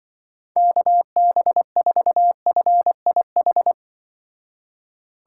Morse code, heard as a keyed tone, about 25 wpm, 700 Hz. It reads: KB4FIH
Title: Speed Racing - USA 2x3 Call Signs 20wpm